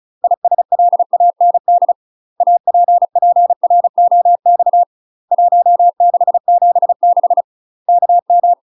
Morse code, heard as a keyed tone, about 36 wpm, 700 Hz. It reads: ISLAND APPROX 1676 KM
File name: UK MASTER QSOs TRAINING / 35WPM / UK MASTER QSOs 0020